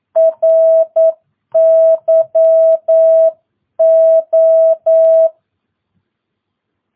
Morse code, heard as a keyed tone, about 9 wpm, 650 Hz. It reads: RYO
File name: jh8cbh / 2022cw